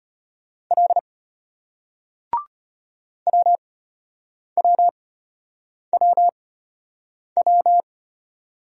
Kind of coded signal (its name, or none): Morse code